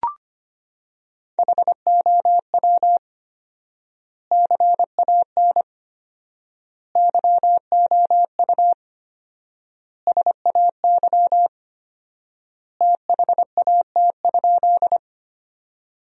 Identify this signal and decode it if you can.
Morse code, heard as a keyed tone, about 25 wpm, 700 Hz. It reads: HOW CAN YOU SAY THAT?
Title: 4x Word-Spacing - Sentences from Top 200 Words - 25wpm